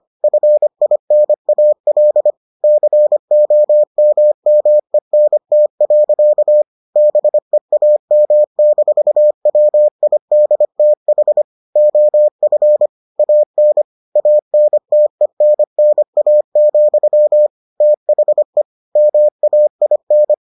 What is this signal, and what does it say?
Morse code, heard as a keyed tone, about 25 wpm, 600 Hz. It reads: FINAL COMMENT. BEAM-WIDTH OF AN ANTENNA, THE MAIN